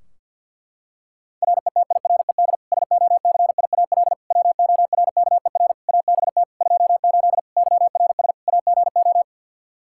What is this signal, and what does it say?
Morse code, heard as a keyed tone, about 51 wpm, 700 Hz. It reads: RETIRED SOCIAL WORKER ABT 28 YRS AGO
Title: K1USN SST QSO Practice 50wpm